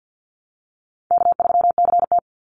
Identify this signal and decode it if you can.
Morse code, heard as a keyed tone, about 50 wpm, 700 Hz. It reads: K3CT
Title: Easy Call Signs - 50wpm